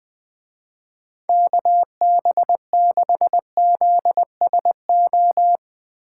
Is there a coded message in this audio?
Yes: Morse code from 1.3 to 5.6 s, heard as a keyed tone, about 20 wpm, 700 Hz. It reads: KB6ZSO